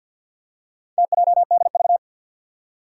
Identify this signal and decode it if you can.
Morse code, heard as a keyed tone, about 52 wpm, 700 Hz. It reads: TJDV
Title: Four Challenging Letter-Number Combinations 50wpm